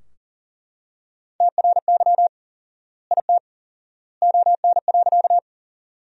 Morse code, heard as a keyed tone, about 40 wpm, 700 Hz. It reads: TRY IT ON.
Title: 4x Word-Spacing - Sentences from Top 200 Words - 40wpm